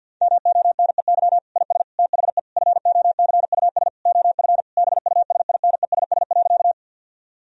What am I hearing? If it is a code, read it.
Morse code, heard as a keyed tone, about 52 wpm, 700 Hz. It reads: MONEY IS THE FOCUS OF BUSINESS.
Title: Sentences from Top 1000 Words - Review Part 4 - 50wpm